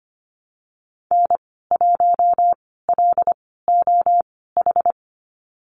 Morse code, heard as a keyed tone, about 25 wpm, spaced as for 16 wpm, 700 Hz. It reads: N1LOH
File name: Lesson 15g - Number 5 - Call Signs 25wpm